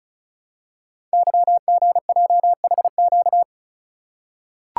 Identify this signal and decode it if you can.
Morse code, heard as a keyed tone, about 36 wpm, 690 Hz. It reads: YGJHQ
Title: Code Groups - 5 Letters - 35wpm